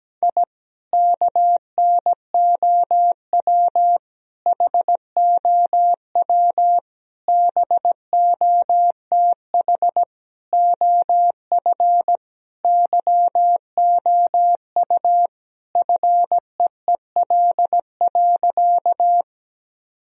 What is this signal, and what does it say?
Morse code, heard as a keyed tone, about 17 wpm, 700 Hz. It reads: I KNOW HOW BOTH OF YOU FEEL.
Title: Sentences from Top 1000 Words - Review Part 1 - 17wpm